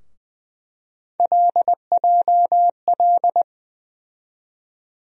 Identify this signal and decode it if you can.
Morse code, heard as a keyed tone, about 20 wpm, 700 Hz. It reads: LJL
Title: USA Call Sign Postfixes - 3 Letters 20wpm